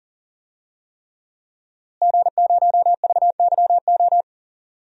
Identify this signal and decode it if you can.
Morse code, heard as a keyed tone, about 40 wpm, 700 Hz. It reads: G0VYO
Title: DXCC-100 Call Signs - 5 Characters - Repeated 3x - 40wpm